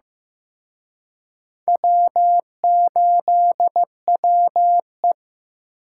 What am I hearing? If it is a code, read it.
Morse code, heard as a keyed tone, about 18 wpm, 700 Hz. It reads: W8WE
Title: Speed Racing - USA Call Signs 15wpm